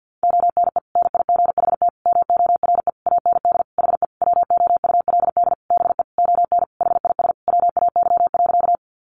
Morse code, heard as a keyed tone, about 50 wpm, 700 Hz. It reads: ONE NIGHT MORE AND HE WOULD BE ON HIS WAY.
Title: Sentences from Top 300 Words 50wpm